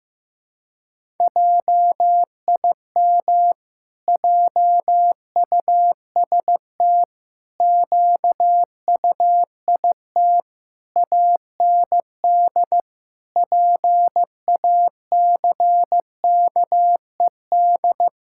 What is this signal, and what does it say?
Morse code, heard as a keyed tone, about 16 wpm, 700 Hz. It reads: JIM JUST QUIT AND PACKED
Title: Pangram Challenge v2 - 15wpm